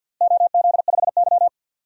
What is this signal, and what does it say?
Morse code, heard as a keyed tone, about 52 wpm, 700 Hz. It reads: OZ5Y
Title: DXCC-100 Call Signs - 4 Characters - Repeated 2x - 50wpm